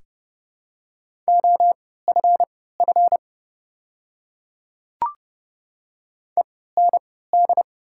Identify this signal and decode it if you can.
Morse code, heard as a keyed tone, about 30 wpm, spaced as for 17 wpm, 690 Hz. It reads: OFF END
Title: 3x Character Spacing - Top 200 Words - 30wpm